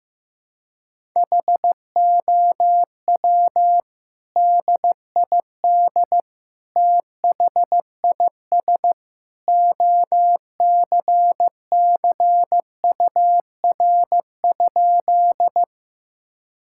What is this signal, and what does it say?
Morse code, heard as a keyed tone, about 15 wpm, 700 Hz. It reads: HOW DID THIS OCCUR?
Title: Sentences from Top 1000 Words - Review Part 6 - 15wpm